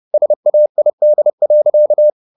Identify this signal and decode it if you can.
Morse code, heard as a keyed tone, about 30 wpm, 600 Hz. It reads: SAID.